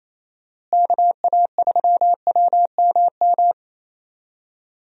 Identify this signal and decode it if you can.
Morse code, heard as a keyed tone, about 27 wpm, 700 Hz. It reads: KA3WMM
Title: USA Call Signs - 6 Characters - Repeated 2x - 28wpm